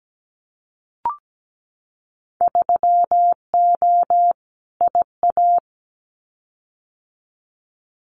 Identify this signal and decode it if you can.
Morse code, heard as a keyed tone, about 17 wpm, 690 Hz. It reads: 3O IA
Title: Winter Field Day Exchanges 17wpm